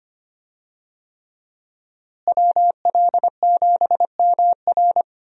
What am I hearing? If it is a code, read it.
Morse code, heard as a keyed tone, about 25 wpm, 700 Hz. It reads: WL7MR
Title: USA Call Signs - 5 Characters - Repeated 3x - 25wpm